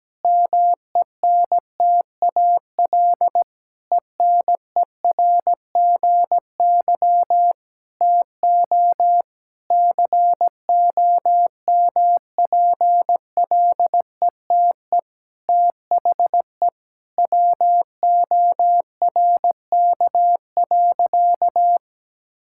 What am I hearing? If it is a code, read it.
Morse code, heard as a keyed tone, about 17 wpm, 700 Hz. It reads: MENTAL ENERGY TO COMPLETE THE WORK.
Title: Sentences from Top 1000 Words - Review Part 6 - 17wpm